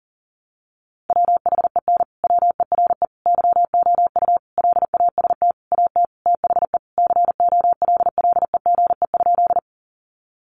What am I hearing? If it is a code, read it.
Morse code, heard as a keyed tone, about 40 wpm, 700 Hz. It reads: WHEN WERE YOU LAST AT THE COLLEGE?
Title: Sentences from Top 1300 Words 40wpm